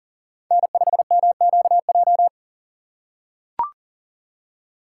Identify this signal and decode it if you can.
Morse code, heard as a keyed tone, about 40 wpm, 700 Hz. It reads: N5MQJ
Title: USA Call Signs - 5 Characters - 6x Character Spacing - 40wpm